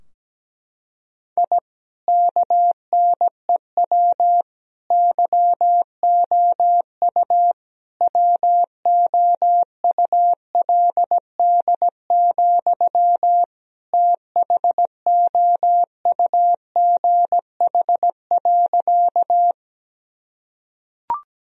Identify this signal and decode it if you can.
Morse code, heard as a keyed tone, about 17 wpm, 700 Hz. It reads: I KNEW YOU WOULD, THOUGH.
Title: Sentences from Top 1000 Words - Review Part 2 - 17wpm